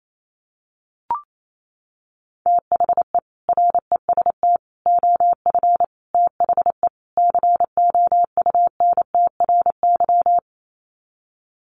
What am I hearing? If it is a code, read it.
Morse code, heard as a keyed tone, about 28 wpm, 700 Hz. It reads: THE REST OF THE COUNTRY
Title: Sets of 5 Words 28wpm